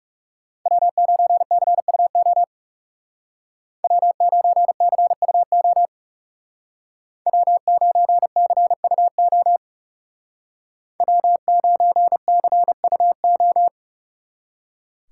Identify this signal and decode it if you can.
Morse code, heard as a keyed tone, about 36 wpm, 700 Hz. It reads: W9CUO W9CUO W9CUO W9CUO
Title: Speed Racing - USA 1x3 Call Signs 30wpm